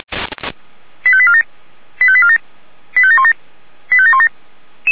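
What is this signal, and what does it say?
Five-tone selective call (ZVEI), heard as quick runs of five tones, tones of about 70 ms. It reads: 85847 85847 85717 85717
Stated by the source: ZVEI five-tone selective call received on an SDR.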